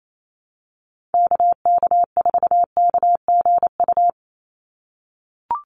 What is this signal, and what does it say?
Morse code, heard as a keyed tone, about 28 wpm, 700 Hz. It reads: KK4KGU
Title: USA Call Signs 28wpm